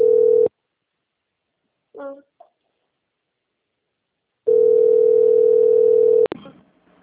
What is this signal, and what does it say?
Ringback tone (North America), 440 + 480 Hz, with speech.
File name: out- - - - - .2044